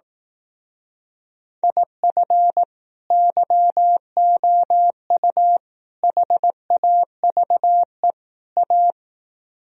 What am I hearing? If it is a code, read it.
Morse code, heard as a keyed tone, about 18 wpm, 700 Hz. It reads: IF YOU HAVE A
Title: Speed-Racing - Sets of 4 Words 15wpm